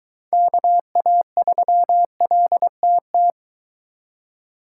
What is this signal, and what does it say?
Morse code, heard as a keyed tone, about 23 wpm, 700 Hz. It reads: KA3LTT